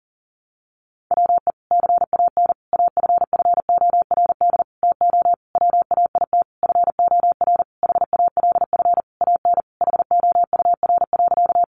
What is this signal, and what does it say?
Morse code, heard as a keyed tone, about 40 wpm, 700 Hz. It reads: WE CAN AFFORD TO WAIT FOR HALF AN HOUR.
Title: Sentences from Top 1100 Words 40wpm